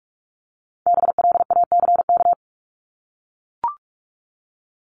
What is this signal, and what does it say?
Morse code, heard as a keyed tone, about 44 wpm, 700 Hz. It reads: BLACK
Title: Top 500 Words Repeated 45wpm